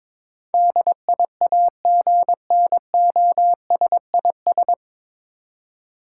Morse code, heard as a keyed tone, about 22 wpm, 700 Hz. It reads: DIAGNOSIS
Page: Morse Code Ninja practice 22wpm